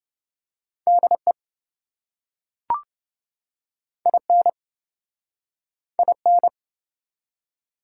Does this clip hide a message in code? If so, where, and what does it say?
Yes: Morse code from 0.9 to 6.5 s, heard as a keyed tone, about 28 wpm, 700 Hz. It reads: DE IN IN